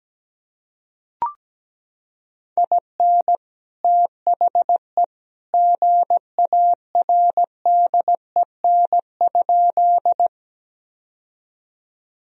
Morse code, heard as a keyed tone, about 17 wpm, 700 Hz. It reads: IN THE GARDEN?
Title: Sentences from Top 1000 Words - Review Part 4 - 17wpm